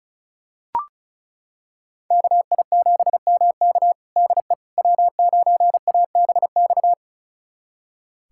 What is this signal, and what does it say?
Morse code, heard as a keyed tone, about 36 wpm, 690 Hz. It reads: KI7MK DE W9ABX
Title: USA Callsign DE Callsign - 5 Characters - 35wpm